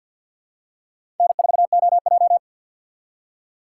Morse code, heard as a keyed tone, about 49 wpm, 700 Hz. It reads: N4OJ
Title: USA Call Signs - 4 Characters - 3x Character Spacing - 50wpm